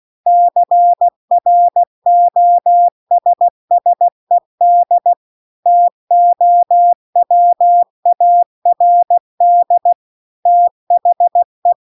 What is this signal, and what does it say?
Morse code, heard as a keyed tone, about 16 wpm, 700 Hz. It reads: CROSSED TOWARD THE